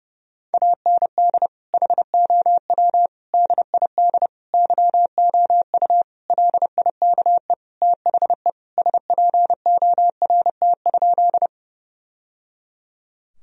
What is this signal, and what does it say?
Morse code, heard as a keyed tone, about 32 wpm, 700 Hz. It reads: AND HOW DID YOU LIKE THE SPORT?
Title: Sentences from Top 1000 Words - Review Part 4 - 30wpm